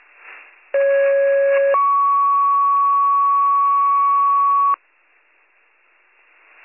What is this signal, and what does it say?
Two-tone page, heard as a one-second tone, then a three-second tone, about 550 then 1120 Hz. It is heard from 0.7 to 4.7 s.